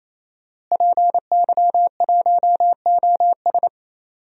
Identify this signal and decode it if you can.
Morse code, heard as a keyed tone, about 27 wpm, 700 Hz. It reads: PY1OS
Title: DXCC-100 Call Signs - 5 Characters - Repeated 3x - 28wpm